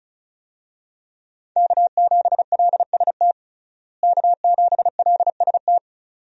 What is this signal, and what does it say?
Morse code, heard as a keyed tone, about 36 wpm, 700 Hz. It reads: K7LST K7LST
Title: USA Call Signs - 5 Characters - Repeated 2x - 35wpm